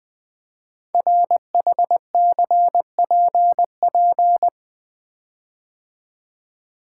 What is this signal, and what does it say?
Morse code, heard as a keyed tone, about 20 wpm, 700 Hz. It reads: RHCPP
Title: Code Groups - 5 Letters - 20wpm